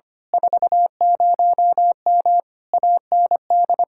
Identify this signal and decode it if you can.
Morse code, heard as a keyed tone, about 24 wpm, 700 Hz. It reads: 40M AND
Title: Speed-Racing Common QSO Phrases Encore - 25wpm